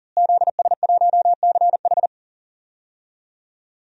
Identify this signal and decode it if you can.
Morse code, heard as a keyed tone, about 40 wpm, 700 Hz. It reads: ZS1CH
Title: DXCC-100 Call Signs - 5 Characters - Repeated 3x - 40wpm